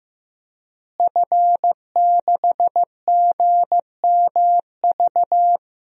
Morse code, heard as a keyed tone, about 15 wpm, 700 Hz. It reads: F6GMV
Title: DXCC-100 Call Signs - 5 Characters - Repeated 3x - 15wpm